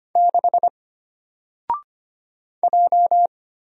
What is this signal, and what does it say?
Morse code, heard as a keyed tone, about 25 wpm, 700 Hz. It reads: 6 J
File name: Single-Letter-Number-25wpm